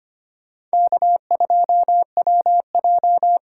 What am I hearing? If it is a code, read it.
Morse code, heard as a keyed tone, about 25 wpm, 700 Hz. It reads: K2WJ
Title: USA Call Signs - 4 Characters - 4x Character Spacing - 25wpm